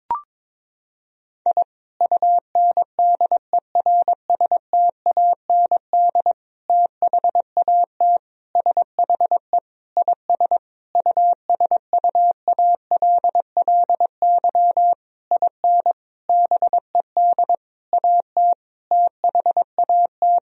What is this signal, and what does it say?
Morse code, heard as a keyed tone, about 22 wpm, 700 Hz. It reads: I UNDERSTAND THAT SHE IS USUALLY IN BED AT THAT
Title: Sentences from Top 1000 Words - Review Part 3 - 22wpm